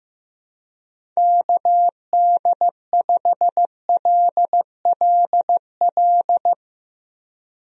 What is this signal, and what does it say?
Morse code, heard as a keyed tone, about 15 wpm, 700 Hz. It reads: KD5LLL
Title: USA Call Signs - 6 Characters - Repeated 3x - 15wpm